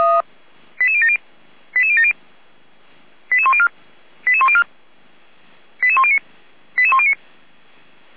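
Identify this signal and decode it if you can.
Five-tone selective call (ZVEI), heard as quick runs of five tones, tones of about 70 ms. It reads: 88070 88070 88104 88104 88109 88109
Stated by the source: ZVEI five-tone call received on SDR, from feuerwehr-orlamuende.org (fire brigade Orlamünde).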